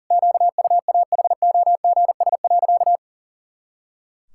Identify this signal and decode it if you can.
Morse code, heard as a keyed tone, about 40 wpm, 700 Hz. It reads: QUAHOGS.
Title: Pangram Challenge v2 - 40wpm